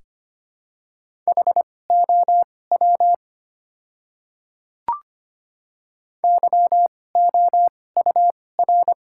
Morse code, heard as a keyed tone, about 25 wpm, spaced as for 18 wpm, 700 Hz. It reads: HOW YOUR
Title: 2x Character Spacing - Top 200 Words - 25wpm